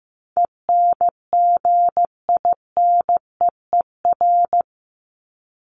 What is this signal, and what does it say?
Morse code, heard as a keyed tone, about 15 wpm, 700 Hz. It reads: ENGINEER
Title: ING Suffix - Top 100 Words with Base Words 15wpm